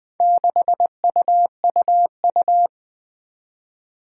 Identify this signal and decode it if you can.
Morse code, heard as a keyed tone, about 20 wpm, 700 Hz. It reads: 6UUU
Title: Four Challenging Letter-Number Combinations 20wpm